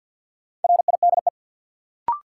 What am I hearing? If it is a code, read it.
Morse code, heard as a keyed tone, about 54 wpm, 700 Hz. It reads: RIDE
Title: Sentences from Top 1200 Words - New Word Drill Alt - 50wpm